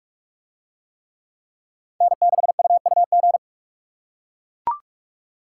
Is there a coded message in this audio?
Yes: Morse code from 2.0 to 3.4 s, heard as a keyed tone, about 42 wpm, 700 Hz. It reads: N6UUG